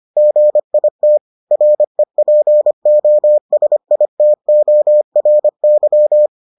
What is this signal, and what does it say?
Morse code, heard as a keyed tone, about 25 wpm, 600 Hz. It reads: GIT REPOSITORY